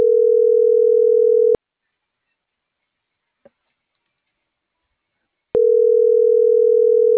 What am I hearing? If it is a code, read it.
Ringback tone (North America), 440 + 480 Hz.